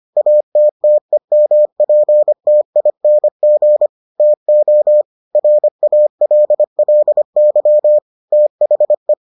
Morse code, heard as a keyed tone, about 25 wpm, 600 Hz. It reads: ATTEMPTING TO RALLY THE